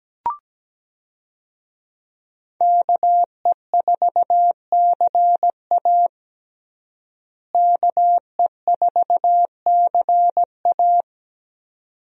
Morse code, heard as a keyed tone, about 17 wpm, 700 Hz. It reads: KE4CA KE4CA
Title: USA Call Signs - 5 Characters - Repeated 2x - 17wpm